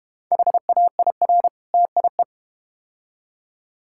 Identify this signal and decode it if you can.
Morse code, heard as a keyed tone, about 31 wpm, 700 Hz. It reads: HAIR TIE
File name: Easy Words - Speed-Racing with Hints - 25wpm